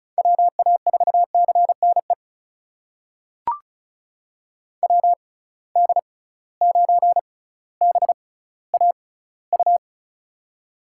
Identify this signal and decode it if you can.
Morse code, heard as a keyed tone, about 36 wpm, spaced as for 18 wpm, 700 Hz. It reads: WA4CNE WD9BAU
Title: USA Call Signs - 6 Characters - 6x Character Spacing - 35wpm